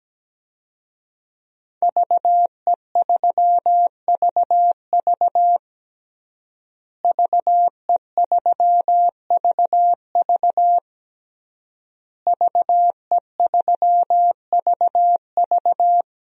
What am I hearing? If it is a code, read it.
Morse code, heard as a keyed tone, about 17 wpm, 700 Hz. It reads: VE3VV VE3VV VE3VV
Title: DXCC-100 Call Signs - 5 Characters - Repeated 3x - 17wpm